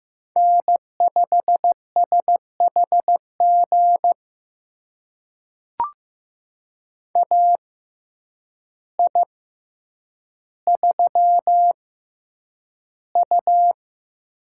Morse code, heard as a keyed tone, about 15 wpm, spaced as for 8 wpm, 700 Hz. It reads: N5SHG AI3U
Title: USA Call Signs - 6x Character Spacing - 15wpm